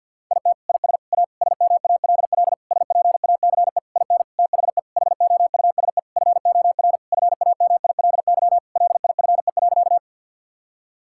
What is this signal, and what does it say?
Morse code, heard as a keyed tone, about 50 wpm, 700 Hz. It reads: IT IS A SMALL SPACE IN THE HOUSE FOR FAMILY LIFE.
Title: Sentences from Top 1000 Words - Review Part 5 - 50wpm